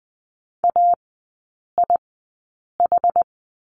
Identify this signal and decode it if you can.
Morse code, heard as a keyed tone, about 20 wpm, spaced as for 8 wpm, 700 Hz. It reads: AIH
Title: 060 Lesson 11e - Letter h - Three Characters 20wpm